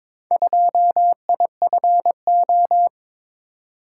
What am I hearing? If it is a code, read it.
Morse code, heard as a keyed tone, about 22 wpm, 700 Hz. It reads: 2IFO
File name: practice / 22wpm / Four Letter-Number Combinations - 22wpm